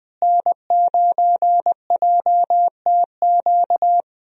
Morse code, heard as a keyed tone, about 20 wpm, 700 Hz. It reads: N9JTQ
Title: USA Call Signs - 5 Characters - Repeated 3x - 20wpm